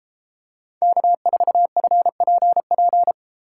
Morse code, heard as a keyed tone, about 30 wpm, 700 Hz. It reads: K4FPP